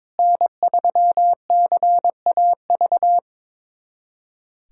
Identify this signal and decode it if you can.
Morse code, heard as a keyed tone, about 22 wpm, 700 Hz. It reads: N3CAV
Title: USA Callsign DE Callsign 22wpm